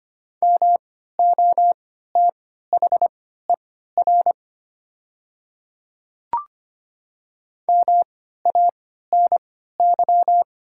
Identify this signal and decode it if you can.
Morse code, heard as a keyed tone, about 25 wpm, spaced as for 14 wpm, 700 Hz. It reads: MOTHER MANY
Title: 3x Character Spacing - Top 200 Words - 25wpm